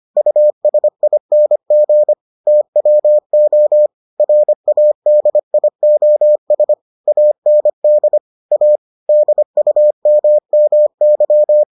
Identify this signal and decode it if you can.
Morse code, heard as a keyed tone, about 25 wpm, 600 Hz. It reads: USING TWO RADIOS AND A DUMMY